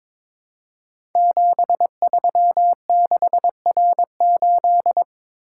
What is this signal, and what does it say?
Morse code, heard as a keyed tone, about 23 wpm, 700 Hz. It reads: 736R8
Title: Code Groups - 5 Letter:Numbers - 22wpm